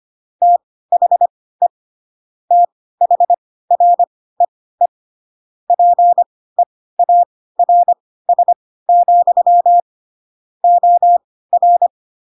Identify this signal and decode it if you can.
Morse code, heard as a keyed tone, about 25 wpm, spaced as for 16 wpm, 700 Hz. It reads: THE THREE PEARS, OR
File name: Pinocchio / Pinocchio ch7 10 25 ik1hge